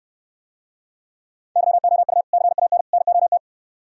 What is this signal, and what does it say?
Morse code, heard as a keyed tone, about 66 wpm, 700 Hz. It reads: YOU CAN NOT